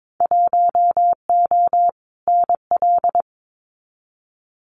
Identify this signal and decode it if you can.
Morse code, heard as a keyed tone, about 22 wpm, 700 Hz. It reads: 1O NL
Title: Winter Field Day Exchanges Encore 22wpm